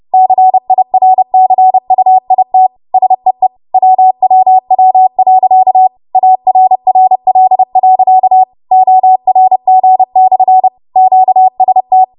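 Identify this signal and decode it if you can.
Morse code, heard as a keyed tone, about 30 wpm, 750 Hz. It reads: CIRCUIT SEE WWW.ARRL.ORG/QST
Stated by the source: W1AW Web Code Practice Files, 30 WPM run from ARRL; text from April 2023 QST.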